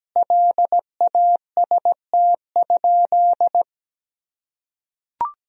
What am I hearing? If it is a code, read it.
Morse code, heard as a keyed tone, about 17 wpm, 700 Hz. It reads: LAST?
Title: Sentences from Top 1000 Words - Review Part 2 - 17wpm